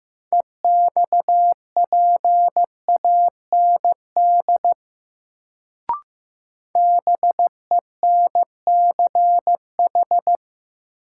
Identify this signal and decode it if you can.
Morse code, heard as a keyed tone, about 15 wpm, 700 Hz. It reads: EXPAND BENCH
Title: Sentences from Top 1400 Words - New Word Drill Alt - 15wpm